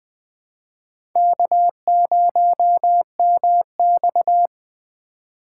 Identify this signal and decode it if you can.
Morse code, heard as a keyed tone, about 20 wpm, 700 Hz. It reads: K0MX
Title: USA Call Signs - 4x Character Spacing - 20wpm